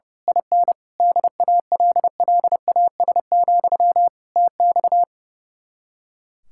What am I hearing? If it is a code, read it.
Morse code, heard as a keyed tone, about 28 wpm, 700 Hz. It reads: IN DALLAS, TX
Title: Speed-Racing Common QSO Phrases Encore - 30wpm